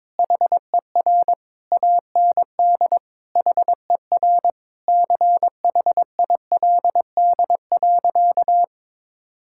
Morse code, heard as a keyed tone, about 22 wpm, 700 Hz. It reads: HER AND HER CHILD.
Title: Sentences from Top 1600 Words 22wpm